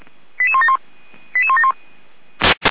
Five-tone selective call (ZVEI), heard as quick runs of five tones, tones of about 80 ms. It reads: 88171 88171